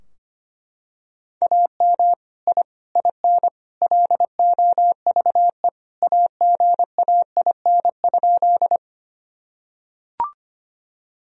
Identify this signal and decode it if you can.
Morse code, heard as a keyed tone, about 25 wpm, 700 Hz. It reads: AM I IN LOVE AGAIN?
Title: Sentences from Top 1000 Words - Review Part 2 - 25wpm